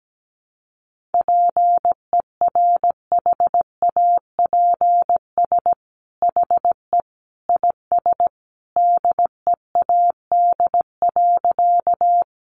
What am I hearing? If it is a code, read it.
Morse code, heard as a keyed tone, about 17 wpm, 700 Hz. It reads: PERHAPS HE IS DEAD.
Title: Sentences from Top 1500 Words 17wpm